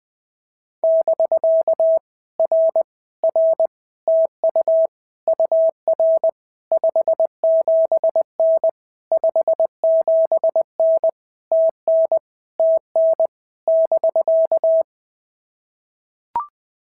Morse code, heard as a keyed tone, about 19 wpm, 650 Hz. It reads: <BK> R R TU UR 57N 57N TN TN <BK>
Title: POTA QSO Practice - 20wpm